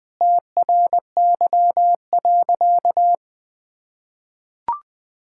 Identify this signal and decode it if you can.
Morse code, heard as a keyed tone, about 21 wpm, 700 Hz. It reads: TRY.
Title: Sentences with 3-Word Phrases - 20wpm